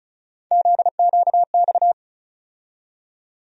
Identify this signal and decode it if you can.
Morse code, heard as a keyed tone, about 36 wpm, 700 Hz. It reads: ZQX
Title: USA Call Sign Postfixes - 3 Letters 35wpm